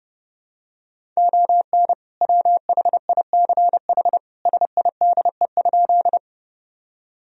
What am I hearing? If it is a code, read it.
Morse code, heard as a keyed tone, about 30 wpm, 700 Hz. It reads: ON WHICH SIDE?
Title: Sentences from Top 1000 Words - Review Part 1 - 30wpm